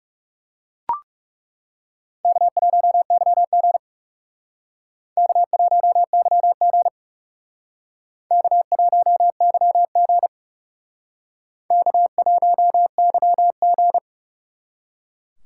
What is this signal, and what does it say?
Morse code, heard as a keyed tone, about 38 wpm, 700 Hz. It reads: K1YG K1YG K1YG K1YG